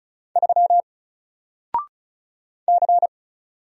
Morse code, heard as a keyed tone, about 36 wpm, 700 Hz. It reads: 3 C
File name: Single-Letter-Number-35wpm